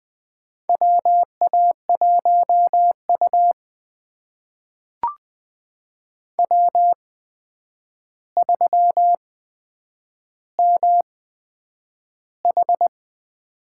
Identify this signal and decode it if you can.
Morse code, heard as a keyed tone, about 20 wpm, spaced as for 9 wpm, 700 Hz. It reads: WA1U W3MH
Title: USA Call Signs - 8x Character Spacing - 20wpm